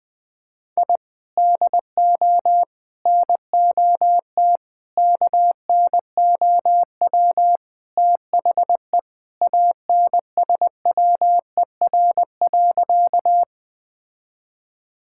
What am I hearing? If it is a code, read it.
Morse code, heard as a keyed tone, about 20 wpm, 700 Hz. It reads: I DO NOT KNOW THE ANSWER.
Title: Sentences with 5-Word Phrases - 20wpm